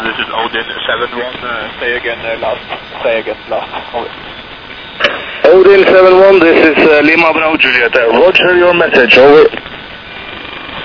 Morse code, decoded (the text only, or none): none